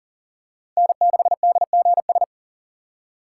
Morse code, heard as a keyed tone, about 40 wpm, 690 Hz. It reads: N6DGS
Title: USA Call Signs - 4x Character Spacing - 40wpm